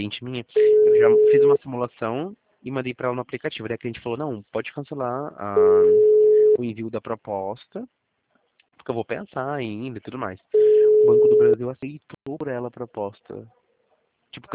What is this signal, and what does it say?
Ringback tone (Europe, 425 Hz), with speech.
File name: out- -2010- - - .92470